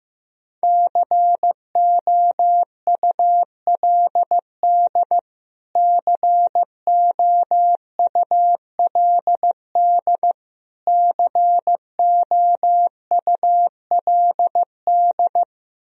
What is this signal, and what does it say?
Morse code, heard as a keyed tone, about 15 wpm, 700 Hz. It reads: COULD COULD COULD
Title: Top 100 Words Repeated 15wpm